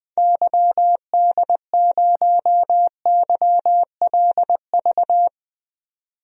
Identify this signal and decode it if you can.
Morse code, heard as a keyed tone, about 20 wpm, 700 Hz. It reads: YD0YLV
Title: DXCC-100 Call Signs - 6 Characters - 20wpm